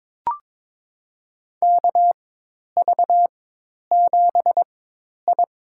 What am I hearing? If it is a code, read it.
Morse code, heard as a keyed tone, about 22 wpm, spaced as for 10 wpm, 700 Hz. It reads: KV7I
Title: USA Call Signs - 4 Characters - 4x Character Spacing - 22wpm